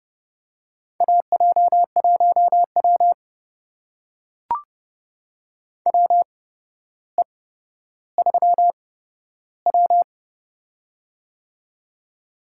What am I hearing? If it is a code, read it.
Morse code, heard as a keyed tone, about 30 wpm, spaced as for 13 wpm, 700 Hz. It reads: AJ1W WE3W
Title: USA Call Signs - 4 Characters - 8x Character Spacing - 30wpm